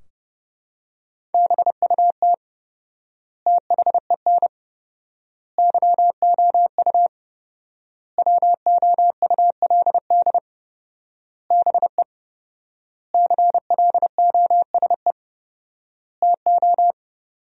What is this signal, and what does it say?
Morse code, heard as a keyed tone, about 30 wpm, 700 Hz. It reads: BUT THEN YOU WOULD BE CLOSE TO
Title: 4x Word-Spacing - Sentences from Top 300 Words - 30wpm